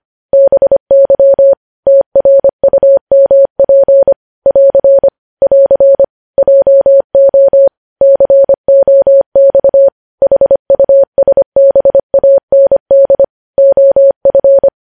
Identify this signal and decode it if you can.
Morse code, heard as a keyed tone, about 25 wpm, 550 Hz. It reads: BY TRUMP <AR> <AR> JO COX HUSBAND OF
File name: podcast / News headlines-1x-25x25